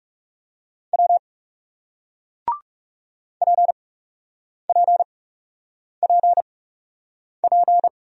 Morse code, heard as a keyed tone, about 40 wpm, spaced as for 8 wpm, 700 Hz. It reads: W PPPP